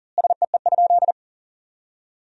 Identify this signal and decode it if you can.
Morse code, heard as a keyed tone, about 40 wpm, 700 Hz. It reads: SEE?
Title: 3x Word-Spacing - Sentences from Top 100 Words - 40wpm